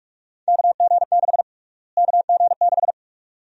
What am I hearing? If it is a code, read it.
Morse code, heard as a keyed tone, about 46 wpm, 700 Hz. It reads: KG6 KG6
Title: USA Call Sign Prefixes - Repeated 2x - 45wpm